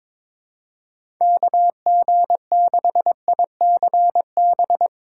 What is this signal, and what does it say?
Morse code, heard as a keyed tone, about 21 wpm, 700 Hz. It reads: KG6ICB
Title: USA Call Signs - 2x Character Spacing - 22wpm